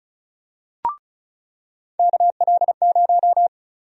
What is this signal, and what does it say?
Morse code, heard as a keyed tone, about 34 wpm, 700 Hz. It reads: KL0
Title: USA Call Sign Prefixes - 35wpm